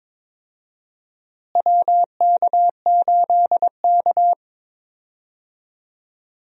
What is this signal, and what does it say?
Morse code, heard as a keyed tone, about 21 wpm, 700 Hz. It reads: WK8K
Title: USA Call Signs - 4 Characters - Repeated 2x - 22wpm